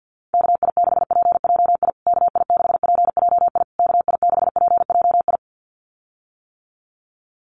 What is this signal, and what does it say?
Morse code, heard as a keyed tone, about 50 wpm, 700 Hz. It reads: KI6PJI KI6PJI KI6PJI
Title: USA Call Signs Repeated 3x 50wpm